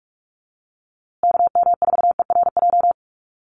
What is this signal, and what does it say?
Morse code, heard as a keyed tone, about 44 wpm, 700 Hz. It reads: KM4ERJ
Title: USA Call Signs Repeated 2x 45wpm